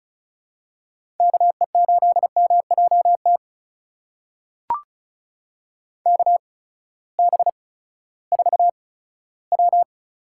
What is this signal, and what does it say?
Morse code, heard as a keyed tone, about 36 wpm, spaced as for 18 wpm, 700 Hz. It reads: KE8MJT KB4W
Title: USA Call Signs - 6 Characters - 8x Character Spacing - 35wpm